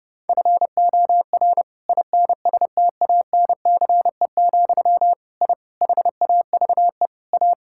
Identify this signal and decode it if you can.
Morse code, heard as a keyed tone, about 30 wpm, 700 Hz. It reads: FOR INSTANCE, I HAVE A
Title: Sentences from Top 1000 Words - Review Part 5 - 30wpm